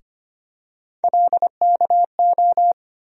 Morse code, heard as a keyed tone, about 25 wpm, 700 Hz. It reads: LKO
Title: USA Call Sign Postfixes - 3 Letters 25wpm